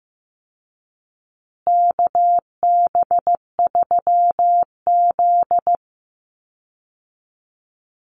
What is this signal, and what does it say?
Morse code, heard as a keyed tone, about 15 wpm, 700 Hz. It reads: KB3Z
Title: Call Signs Repeated 2x - 15wpm